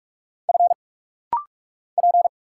Morse code, heard as a keyed tone, about 44 wpm, 700 Hz. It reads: F P
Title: Single Letter-Number Alt - Rapid-Fire 45wpm